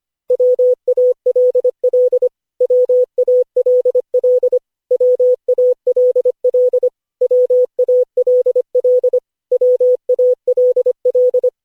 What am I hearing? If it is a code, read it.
Morse code, heard as a keyed tone, about 25 wpm, 500 Hz. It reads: WALL WALL WALL WALL WALL
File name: cw / Session5 cw training